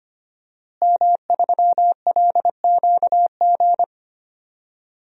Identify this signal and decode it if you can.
Morse code, heard as a keyed tone, about 25 wpm, 700 Hz. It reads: M3LQG
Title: DXCC-100 Call Signs - 5 Characters - Repeated 2x - 25wpm